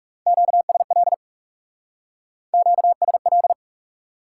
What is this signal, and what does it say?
Morse code, heard as a keyed tone, about 35 wpm, 700 Hz. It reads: QSL QSL
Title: Speed-Racing Top 180 QSO Elements 30wpm